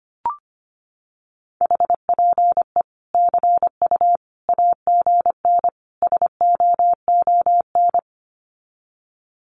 Morse code, heard as a keyed tone, about 25 wpm, 700 Hz. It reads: HPE CU AGN SOON
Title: Common QSO Phrases 25wpm